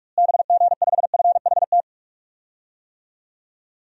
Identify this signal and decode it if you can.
Morse code, heard as a keyed tone, about 46 wpm, 700 Hz. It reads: BG5FHT
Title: DXCC-100 Call Signs - 6 Characters - Repeated 3x - 45wpm